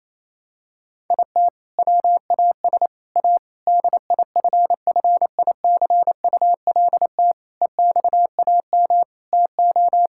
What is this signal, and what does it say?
Morse code, heard as a keyed tone, about 28 wpm, 700 Hz. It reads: IT WAS A DIFFICULT EXAM TO
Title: Sentences from Top 1000 Words - Review Part 5 - 28wpm